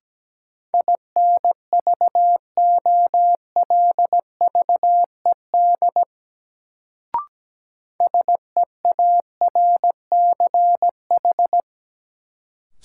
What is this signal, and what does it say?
Morse code, heard as a keyed tone, about 17 wpm, 700 Hz. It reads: INVOLVED SEARCH
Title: Sentences from Top 800 Words - New Word Drill - 17wpm